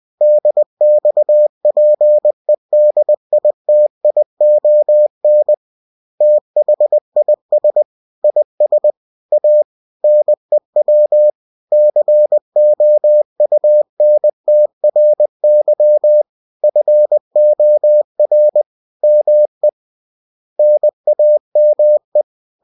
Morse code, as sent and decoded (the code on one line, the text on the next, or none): -.. -..- .--. -.. .. - .. --- -. / - .... .. ... / .. ... / .- / -. .-- / -.-. --- ..- -. - .-. -.-- / ..-. --- .-. / -- . / -. .- -- .
DXPEDITION THIS IS A NEW COUNTRY FOR ME NAME